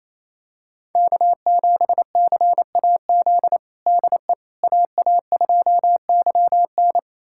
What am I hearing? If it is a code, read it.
Morse code, heard as a keyed tone, about 28 wpm, 700 Hz. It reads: K7CAZ DE AA2YN
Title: USA Callsign DE Callsign - 5 Characters - 28wpm